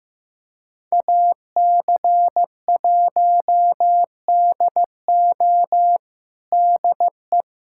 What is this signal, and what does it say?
Morse code, heard as a keyed tone, about 15 wpm, 700 Hz. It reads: AC1DO DE
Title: USA Callsign DE Callsign - 5 Characters - 15wpm